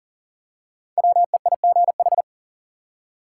Morse code, heard as a keyed tone, about 40 wpm, 700 Hz. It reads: WEIGH
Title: Five Letter Words - 40wpm